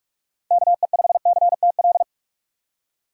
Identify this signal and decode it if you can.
Morse code, heard as a keyed tone, about 46 wpm, 700 Hz. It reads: KE5CTL
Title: USA Call Signs - 6 Characters - 2x Character Spacing - 45wpm